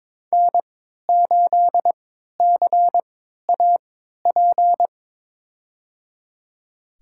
Morse code, heard as a keyed tone, about 22 wpm, spaced as for 12 wpm, 700 Hz. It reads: N8CAP
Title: USA Call Signs - 5 Characters - 3x Character Spacing - 22wpm